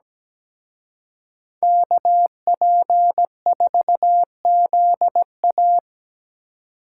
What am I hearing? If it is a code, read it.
Morse code, heard as a keyed tone, about 20 wpm, 700 Hz. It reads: KP4ZA
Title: Speed Racing - USA 2x2 Call Signs 17wpm